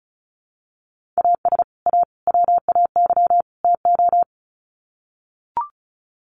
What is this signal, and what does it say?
Morse code, heard as a keyed tone, about 34 wpm, 700 Hz. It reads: AS A WAY TO
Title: Sets of 4 Words 35wpm